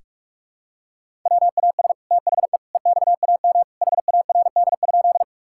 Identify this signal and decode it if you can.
Morse code, heard as a keyed tone, about 46 wpm, 700 Hz. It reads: WAS THE EXAM HARD?
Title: Sentences from Top 1000 Words - Review Part 5 - 45wpm